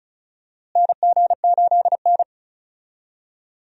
Morse code, heard as a keyed tone, about 36 wpm, 700 Hz. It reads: NG8N
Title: USA Call Signs - 4 Characters - Repeated 2x - 35wpm